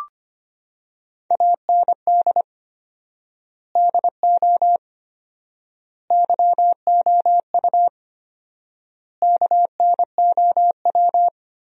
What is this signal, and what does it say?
Morse code, heard as a keyed tone, about 25 wpm, 700 Hz. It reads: AND DO YOU KNOW
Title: 4x Word-Spacing - Sentences from Top 200 Words - 25wpm